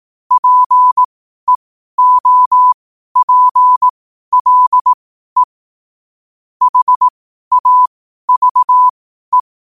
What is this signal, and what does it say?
Morse code, heard as a keyed tone, about 18 wpm, spaced as for 12 wpm, 1000 Hz. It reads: PEOPLE HAVE
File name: qotdfiles / 12wpm